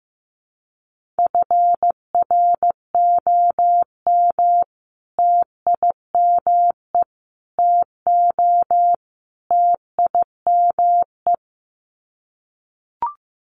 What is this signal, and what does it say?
Morse code, heard as a keyed tone, about 15 wpm, 700 Hz. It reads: FROM TIME TO TIME
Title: Sets of 4 Words 15wpm